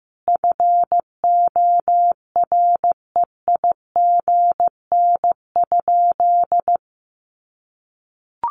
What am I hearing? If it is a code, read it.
Morse code, heard as a keyed tone, about 15 wpm, 700 Hz. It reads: FOREIGN?
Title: Sentences from Top 1400 Words 15wpm